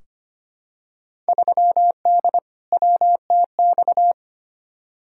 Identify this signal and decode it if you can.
Morse code, heard as a keyed tone, about 25 wpm, 700 Hz. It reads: 3D WTX
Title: Field Day Exchanges Encore - 25wpm